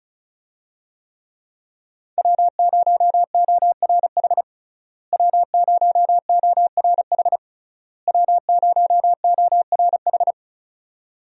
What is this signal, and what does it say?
Morse code, heard as a keyed tone, about 34 wpm, 700 Hz. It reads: W0ORH W0ORH W0ORH
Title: USA Call Signs - 5 Characters - Repeated 3x - 35wpm